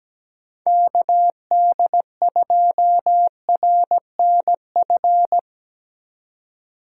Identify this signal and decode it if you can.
Morse code, heard as a keyed tone, about 17 wpm, 700 Hz. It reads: KD2RNF
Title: USA Call Signs - 6 Characters - 4x Character Spacing - 17wpm